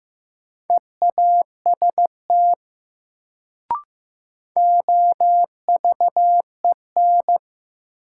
Morse code, heard as a keyed tone, about 15 wpm, 700 Hz. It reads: EAST OVEN
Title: Sentences from Top 1200 Words - New Word Drill Alt - 15wpm